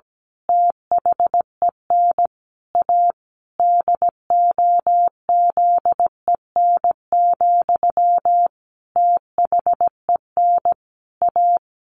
Morse code, heard as a keyed tone, about 17 wpm, 700 Hz. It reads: THEN A DOZEN, THEN A
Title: A Sound Of Thunder 17wpm